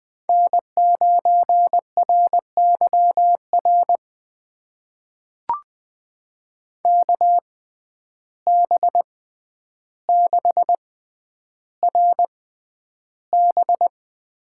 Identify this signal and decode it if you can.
Morse code, heard as a keyed tone, about 20 wpm, spaced as for 10 wpm, 700 Hz. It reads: N9RYR KB6RB
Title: USA Call Signs - 6x Character Spacing - 20wpm